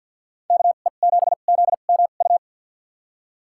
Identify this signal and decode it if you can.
Morse code, heard as a keyed tone, about 53 wpm, spaced as for 37 wpm, 690 Hz. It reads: KE7ZMU
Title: USA Call Signs - 2x Character Spacing - 50wpm